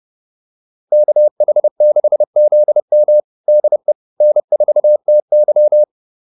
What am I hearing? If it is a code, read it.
Morse code, heard as a keyed tone, about 30 wpm, 600 Hz. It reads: KH6ZM DE N4TY